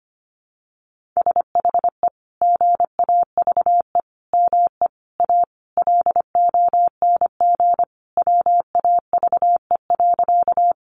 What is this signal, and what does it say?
Morse code, heard as a keyed tone, about 25 wpm, 700 Hz. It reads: SHE GAVE ME A LONG WAVE.
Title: Sentences from Top 1400 Words 25wpm